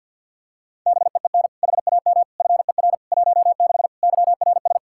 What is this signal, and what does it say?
Morse code, heard as a keyed tone, about 47 wpm, 700 Hz. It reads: BEEN HAM FER 16 YRS
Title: Practice QSO - Exchange 3 - More Info - 50wpm